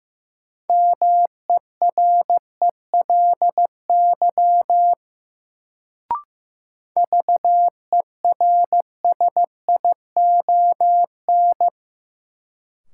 Morse code, heard as a keyed tone, about 15 wpm, 700 Hz. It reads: MERELY VERSION
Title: Sentences from Top 1400 Words - New Word Drill - 15wpm